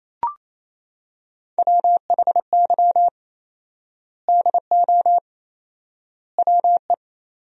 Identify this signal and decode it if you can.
Morse code, heard as a keyed tone, about 28 wpm, 700 Hz. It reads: WHY DO WE
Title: 4x Word-Spacing - Sentences from Top 400 Words - 28wpm